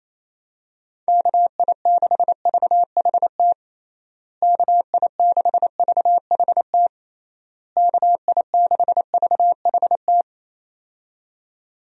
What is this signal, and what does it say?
Morse code, heard as a keyed tone, about 27 wpm, 700 Hz. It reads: KI6VHT KI6VHT KI6VHT
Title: USA Call Signs - 6 Characters - Repeated 3x - 28wpm